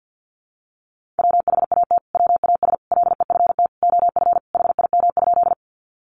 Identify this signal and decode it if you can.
Morse code, heard as a keyed tone, about 50 wpm, 700 Hz. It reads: WHAT WAS LEFT OF HIM?
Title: Sentences from Top 300 Words 50wpm